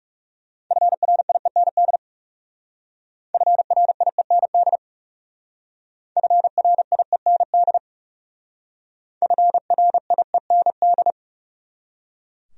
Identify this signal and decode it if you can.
Morse code, heard as a keyed tone, about 36 wpm, 700 Hz. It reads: FRIEND FRIEND FRIEND FRIEND
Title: Speed-Racing Top 500 Words 30wpm